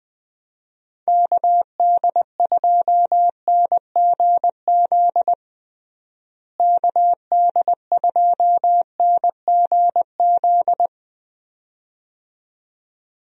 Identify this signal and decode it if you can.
Morse code, heard as a keyed tone, about 20 wpm, 700 Hz. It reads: KD2NGZ KD2NGZ
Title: USA Call Signs - 6 Characters - Repeated 2x - 20wpm